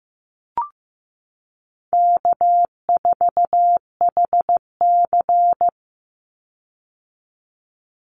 Morse code, heard as a keyed tone, about 15 wpm, 700 Hz. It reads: K4HC
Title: Easy Call Signs - 15wpm